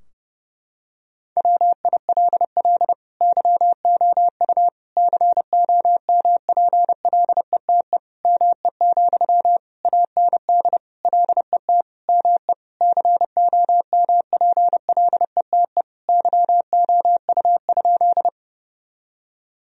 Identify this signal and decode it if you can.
Morse code, heard as a keyed tone, about 30 wpm, 700 Hz. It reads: WILL YOU COMPLETE ME, AND LET ME COMPLETE YOU?
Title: Sentences from Top 1000 Words - Review Part 2 - 30wpm